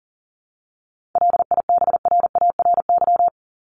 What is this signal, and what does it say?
Morse code, heard as a keyed tone, about 37 wpm, 700 Hz. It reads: LIBRARY
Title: Top 501-750 Words 40wpm